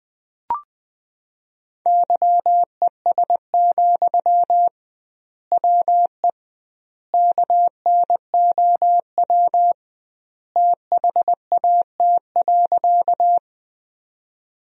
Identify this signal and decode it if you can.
Morse code, heard as a keyed tone, about 21 wpm, 700 Hz. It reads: YES, WE KNOW THAT.
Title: 2x Word-Spacing - Sentences from Top 500 Words - 20wpm